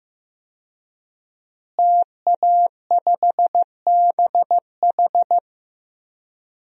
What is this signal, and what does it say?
Morse code, heard as a keyed tone, about 15 wpm, 700 Hz. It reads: TA5BH
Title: DXCC-100 Call Signs - 5 Characters - Repeated 2x - 15wpm